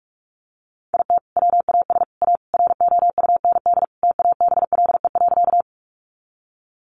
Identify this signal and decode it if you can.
Morse code, heard as a keyed tone, about 44 wpm, 700 Hz. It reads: IT WAS A ROUND TABLE.
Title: Sentences from Top 400 Words 45wpm